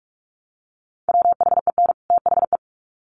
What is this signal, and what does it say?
Morse code, heard as a keyed tone, about 44 wpm, 700 Hz. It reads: WHEN THE
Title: Sets of 2 Words 45wpm